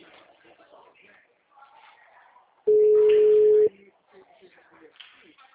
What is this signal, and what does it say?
Ringback tone (Europe, 425 Hz), with speech.